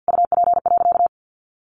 Morse code, heard as a keyed tone, about 50 wpm, 700 Hz. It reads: UP.
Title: Sentences from Top 1300 Words 50wpm